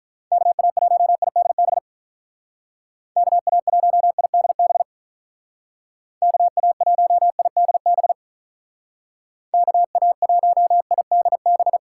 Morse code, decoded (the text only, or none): KA1IDB KA1IDB KA1IDB KA1IDB